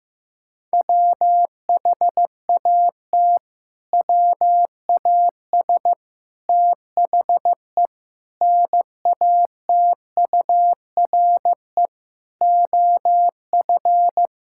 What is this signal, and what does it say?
Morse code, heard as a keyed tone, about 15 wpm, 700 Hz. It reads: WHAT WAS THE NATURE OF
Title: Sentences from Top 1600 Words 15wpm